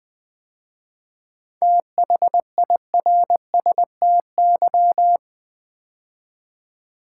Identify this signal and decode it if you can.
Morse code, heard as a keyed tone, about 20 wpm, 700 Hz. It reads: THIRSTY
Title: 7 Letter Words - First 3 Letters Given - 20wpm